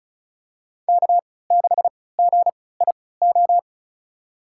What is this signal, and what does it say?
Morse code, heard as a keyed tone, about 36 wpm, spaced as for 20 wpm, 700 Hz. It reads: K6GIO
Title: USA Call Signs - 3x Character Spacing - 35wpm